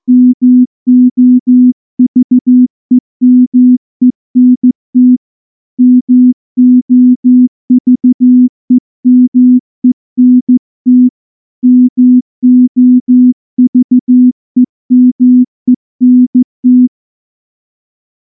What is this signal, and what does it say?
Morse code, heard as a keyed tone, about 14 wpm, 250 Hz. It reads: MOVEMENT MOVEMENT MOVEMENT
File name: morse / morse-training-15WPM